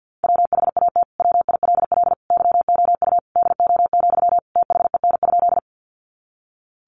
Morse code, heard as a keyed tone, about 50 wpm, 700 Hz. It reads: WHAT WILL YOU DO, THEN?
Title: Sentences from Top 100 Words 50wpm